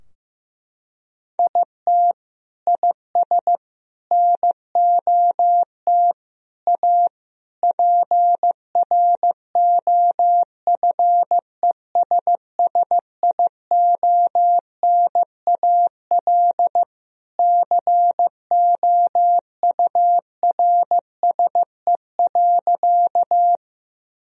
Morse code, heard as a keyed tone, about 15 wpm, 700 Hz. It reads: IT IS NOT A PROFESSIONAL COURSE.
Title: Sentences from Top 1000 Words - Review Part 4 - 15wpm